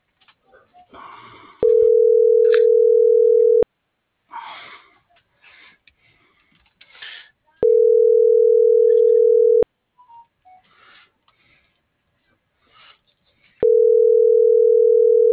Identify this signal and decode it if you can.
Ringback tone (North America), 440 + 480 Hz.